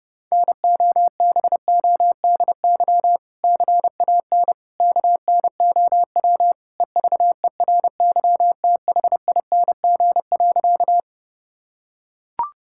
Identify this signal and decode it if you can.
Morse code, heard as a keyed tone, about 30 wpm, 700 Hz. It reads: NOBODY CAN KNOW EVERYTHING.
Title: Sentences from Top 1600 Words 30wpm